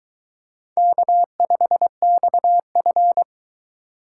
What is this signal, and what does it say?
Morse code, heard as a keyed tone, about 22 wpm, 700 Hz. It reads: K5XF